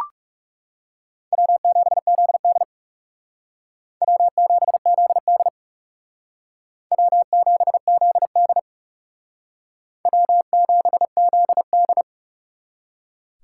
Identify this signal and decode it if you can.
Morse code, heard as a keyed tone, about 37 wpm, 700 Hz. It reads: W7ZD W7ZD W7ZD W7ZD
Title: Speed Racing - USA Call Signs 30wpm